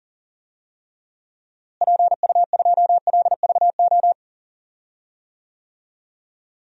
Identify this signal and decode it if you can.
Morse code, heard as a keyed tone, about 40 wpm, 700 Hz. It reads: PU2LVO
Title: DXCC-100 Call Signs - 6 Characters - Repeated 3x - 40wpm